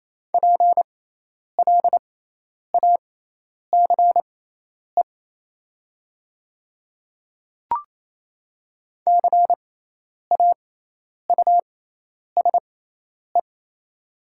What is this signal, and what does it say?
Morse code, heard as a keyed tone, about 28 wpm, spaced as for 10 wpm, 700 Hz. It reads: PLACE CAUSE
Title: 6x Character Spacing - Top 300 Words - 28wpm